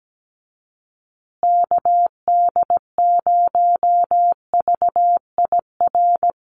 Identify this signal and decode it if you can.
Morse code, heard as a keyed tone, about 17 wpm, 700 Hz. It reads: KD0VIR
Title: USA Call Signs Repeated 3x 17wpm